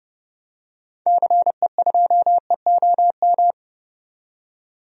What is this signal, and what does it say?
Morse code, heard as a keyed tone, about 30 wpm, 700 Hz. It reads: CE2EOM
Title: DXCC-100 Call Signs - 6 Characters - 30wpm